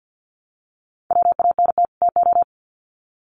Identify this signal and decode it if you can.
Morse code, heard as a keyed tone, about 50 wpm, 700 Hz. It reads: WANT TO
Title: Sets of 2 Words 50wpm